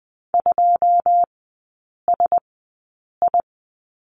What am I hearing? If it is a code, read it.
Morse code, heard as a keyed tone, about 20 wpm, spaced as for 8 wpm, 700 Hz. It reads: 2SI
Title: 100 Lesson 17e - Letter u - Three Characters 20wpm